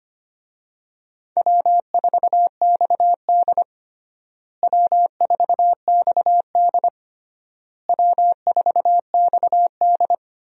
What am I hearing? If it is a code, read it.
Morse code, heard as a keyed tone, about 25 wpm, 700 Hz. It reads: W4XD W4XD W4XD
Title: USA Call Signs - 4 Characters - Repeated 3x - 25wpm